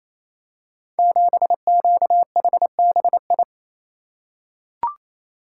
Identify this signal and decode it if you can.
Morse code, heard as a keyed tone, about 28 wpm, 700 Hz. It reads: 7QHBI
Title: Code Groups - 5 Letter:Numbers - 28wpm